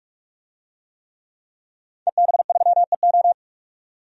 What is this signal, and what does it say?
Morse code, heard as a keyed tone, about 46 wpm, 700 Hz. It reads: EB3EO
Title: DXCC-100 Call Signs - 5 Characters - 45wpm